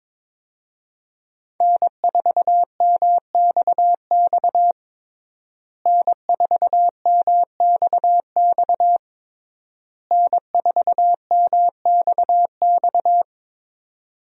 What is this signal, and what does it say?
Morse code, heard as a keyed tone, about 22 wpm, 700 Hz. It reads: N4MXX N4MXX N4MXX
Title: USA Call Signs - 5 Characters - Repeated 3x - 22wpm